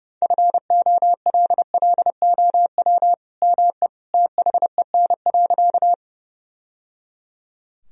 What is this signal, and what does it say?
Morse code, heard as a keyed tone, about 30 wpm, 700 Hz. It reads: FOLLOW ME THEN.
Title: Sentences from Top 1000 Words - Review Part 1 - 30wpm